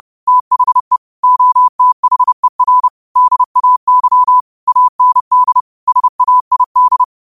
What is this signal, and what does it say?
Morse code, heard as a keyed tone, about 30 wpm, 1000 Hz. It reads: THE OTHER DAY AND SAID
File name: qotdfiles / 30wpm